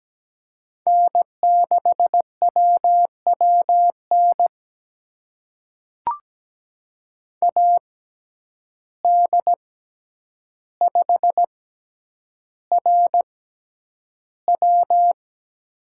Morse code, heard as a keyed tone, about 17 wpm, spaced as for 9 wpm, 700 Hz. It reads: N6WWN AD5RW
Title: USA Call Signs - 5 Characters - 6x Character Spacing - 17wpm